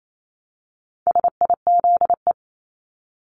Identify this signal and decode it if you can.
Morse code, heard as a keyed tone, about 28 wpm, 700 Hz. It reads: SIZE
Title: Top 500 Words 28wpm